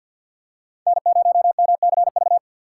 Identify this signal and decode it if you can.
Morse code, heard as a keyed tone, about 50 wpm, 700 Hz. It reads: N0MCV
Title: USA Call Signs - 5 Characters - 3x Character Spacing - 50wpm